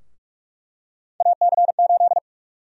Morse code, heard as a keyed tone, about 46 wpm, 700 Hz. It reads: AC8
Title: USA Call Sign Prefixes - Repeated 2x - 45wpm